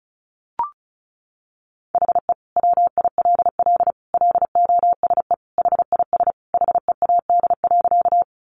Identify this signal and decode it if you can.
Morse code, heard as a keyed tone, about 34 wpm, 700 Hz. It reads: HE WILL LOSE HIS HEAD.
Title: Sentences from Top 1100 Words 35wpm